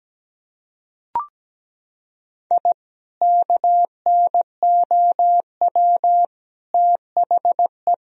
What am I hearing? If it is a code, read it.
Morse code, heard as a keyed tone, about 17 wpm, 700 Hz. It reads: I KNOW THE
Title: Sentences from Top 1000 Words - Review Part 2 - 17wpm